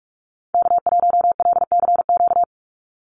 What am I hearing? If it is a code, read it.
Morse code, heard as a keyed tone, about 46 wpm, 690 Hz. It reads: K1LCQ
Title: USA Call Signs 45wpm